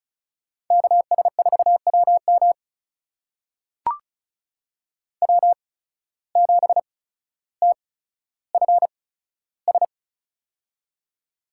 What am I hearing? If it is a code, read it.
Morse code, heard as a keyed tone, about 34 wpm, spaced as for 15 wpm, 700 Hz. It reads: KS4WM W7TFS
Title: USA Call Signs - 5 Characters - 8x Character Spacing - 35wpm